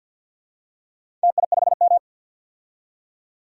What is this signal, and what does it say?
Morse code, heard as a keyed tone, about 53 wpm, 700 Hz. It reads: TI5M
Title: DXCC-100 Call Signs - 4 Characters - 50wpm